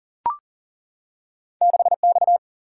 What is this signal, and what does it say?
Morse code, heard as a keyed tone, about 38 wpm, 700 Hz. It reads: 6 X
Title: Single-Letter Single-Number 40wpm